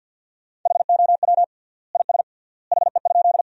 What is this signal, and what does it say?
Morse code, heard as a keyed tone, about 53 wpm, 700 Hz. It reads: HOW IS HE?
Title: 3x Word-Spacing - Sentences from Top 100 Words - 50wpm